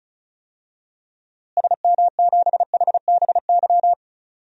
Morse code, heard as a keyed tone, about 34 wpm, 700 Hz. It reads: SM7HBY
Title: DXCC-100 Call Signs - 6 Characters - Repeated 3x - 35wpm